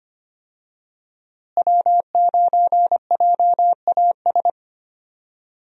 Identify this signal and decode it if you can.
Morse code, heard as a keyed tone, about 25 wpm, 700 Hz. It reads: W9JAS
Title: USA Call Signs - 5 Characters - Repeated 3x - 25wpm